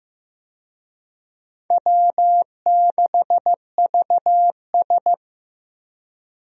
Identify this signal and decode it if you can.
Morse code, heard as a keyed tone, about 15 wpm, 700 Hz. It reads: W6VS
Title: USA Call Signs - 4 Characters - Repeated 3x - 15wpm